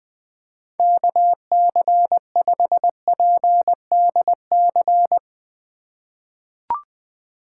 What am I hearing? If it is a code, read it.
Morse code, heard as a keyed tone, about 21 wpm, 700 Hz. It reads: KC5PDC
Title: USA Call Signs - 6 Characters - 2x Character Spacing - 20wpm